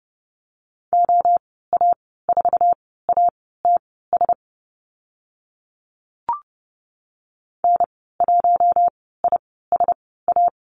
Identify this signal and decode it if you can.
Morse code, heard as a keyed tone, about 30 wpm, spaced as for 17 wpm, 700 Hz. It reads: OA4ATS N1ISA
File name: Lesson 15g - Number 5 - Call Signs 30wpm